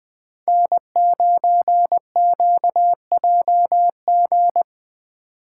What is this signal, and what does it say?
Morse code, heard as a keyed tone, about 20 wpm, 700 Hz. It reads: N9QJG